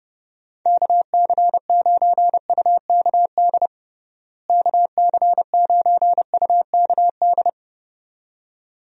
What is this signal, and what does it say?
Morse code, heard as a keyed tone, about 32 wpm, 700 Hz. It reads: KC9UKD KC9UKD
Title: USA Call Signs - 6 Characters - Repeated 2x - 30wpm